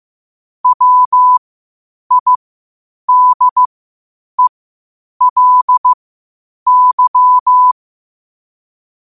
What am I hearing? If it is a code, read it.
Morse code, heard as a keyed tone, about 15 wpm, spaced as for 8 wpm, 1000 Hz. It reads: WIDELY
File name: qotdfiles / 8wpm